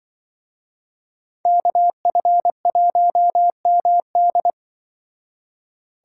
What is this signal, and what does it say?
Morse code, heard as a keyed tone, about 24 wpm, 700 Hz. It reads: KF1MD
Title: Speed Racing - USA 2x2 Call Signs 20wpm